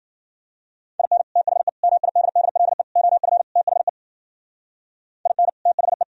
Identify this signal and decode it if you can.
Morse code, heard as a keyed tone, about 50 wpm, 700 Hz. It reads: IN THE MIDDLE OF THE IN THE
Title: Speed-Racing - Sets of 5 Words 40wpm